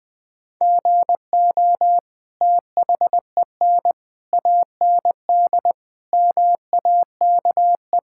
Morse code, heard as a keyed tone, about 20 wpm, 700 Hz. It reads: GO THEN AND MAKE
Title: Sentences from Top 1000 Words - Review Part 5 - 20wpm